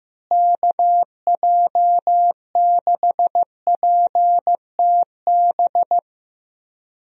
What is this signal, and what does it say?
Morse code, heard as a keyed tone, about 15 wpm, 700 Hz. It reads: KJ6PTB
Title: USA Call Signs - 6 Characters - Repeated 3x - 15wpm